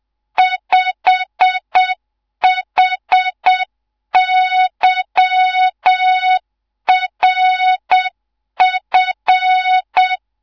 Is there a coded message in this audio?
Yes: Morse code from 0.4 to 10.1 s, heard as a keyed tone, about 7 wpm, 3700 Hz. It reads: 5HYRF